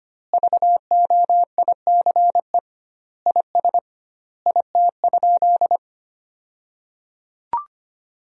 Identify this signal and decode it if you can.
Morse code, heard as a keyed tone, about 25 wpm, 700 Hz. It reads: VOICE IS IT?
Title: 2x Word-Spacing - Sentences from Top 500 Words - 25wpm